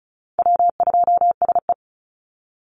Morse code, heard as a keyed tone, about 34 wpm, 700 Hz. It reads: W2SE
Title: NA SOTA Chaser Callsigns 35wpm